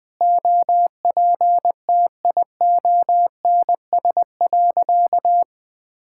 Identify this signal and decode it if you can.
Morse code, heard as a keyed tone, about 20 wpm, 700 Hz. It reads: OPTIONS.
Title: Sentences with 3-Word Phrases - Spoken Beginning - 20wpm